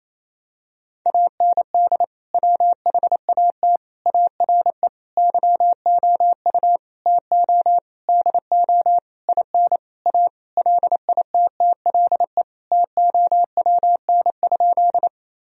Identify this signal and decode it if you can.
Morse code, heard as a keyed tone, about 28 wpm, 700 Hz. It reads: AND WHAT ARE YOU TO DO IN A LITTLE TOWN?
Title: Sentences from Top 1000 Words - Review Part 2 - 28wpm